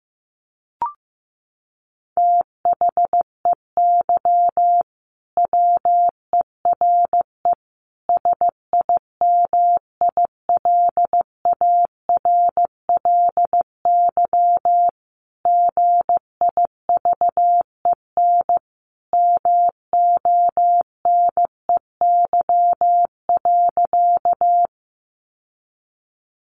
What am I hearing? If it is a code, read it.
Morse code, heard as a keyed tone, about 15 wpm, 700 Hz. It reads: THEY WERE SIMILARLY GIVEN MONEY.
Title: Sentences from Top 1400 Words 15wpm